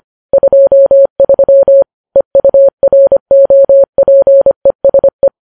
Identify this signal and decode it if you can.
Morse code, heard as a keyed tone, about 25 wpm, 550 Hz. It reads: 23 EUROPESE
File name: Files Web tls / Morse on9fv / morse25kps